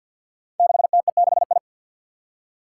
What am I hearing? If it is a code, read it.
Morse code, heard as a keyed tone, about 47 wpm, 700 Hz. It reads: 6TE6I
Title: Five Challenging Letter-Number Combinations 50wpm